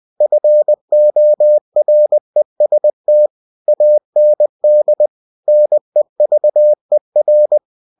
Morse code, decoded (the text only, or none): FOREST AND NEVER